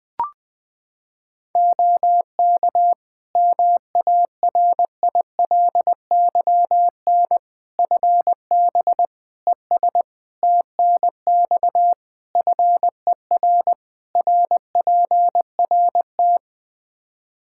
Morse code, heard as a keyed tone, about 20 wpm, 700 Hz. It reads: OK MARILYN FB ES TNX FER RPRT
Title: Practice QSO - Exchange 2 - Info - 20wpm